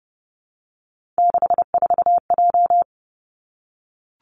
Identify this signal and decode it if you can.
Morse code, heard as a keyed tone, about 30 wpm, 700 Hz. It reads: 64J
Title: Three Challenging Letter-Number Combinations 30wpm